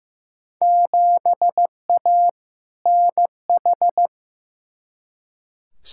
Morse code, heard as a keyed tone, about 15 wpm, 700 Hz. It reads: 7A NH
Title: Field Day Exchanges - 15wpm